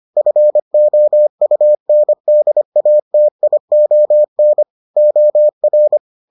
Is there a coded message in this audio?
Yes: Morse code from 0.2 to 6.0 s, heard as a keyed tone, about 25 wpm, 600 Hz. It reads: FOUNDATION OR